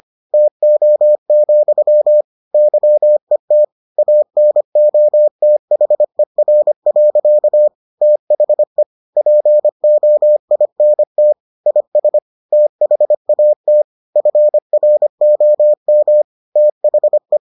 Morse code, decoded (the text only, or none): TO, YET ANOTHER. THE POINT IS THAT FROM THE